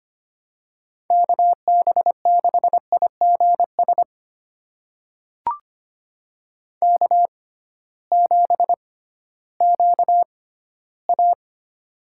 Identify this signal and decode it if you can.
Morse code, heard as a keyed tone, about 25 wpm, spaced as for 14 wpm, 700 Hz. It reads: KB6IGS K7QA
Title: USA Call Signs - 6x Character Spacing - 25wpm